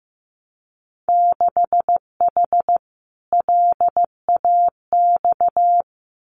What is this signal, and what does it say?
Morse code, heard as a keyed tone, about 15 wpm, 700 Hz. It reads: 6H LAX
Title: Winter Field Day Exchanges Encore 15wpm